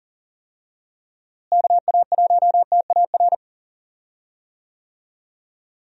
Morse code, heard as a keyed tone, about 40 wpm, 700 Hz. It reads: KA1TAR
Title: USA Call Signs - 6 Characters - Repeated 2x - 40wpm